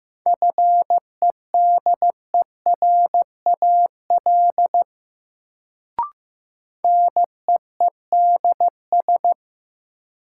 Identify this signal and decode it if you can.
Morse code, heard as a keyed tone, about 15 wpm, 700 Hz. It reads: FEDERAL NEEDS
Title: Sets of 3 Words - New Word Drill - 15wpm